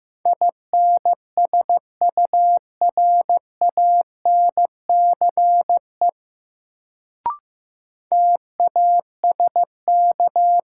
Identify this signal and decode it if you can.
Morse code, heard as a keyed tone, about 15 wpm, 700 Hz. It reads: INSURANCE TASK
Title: Sentences from Top 1100 Words - New Word Drill Alt - 15wpm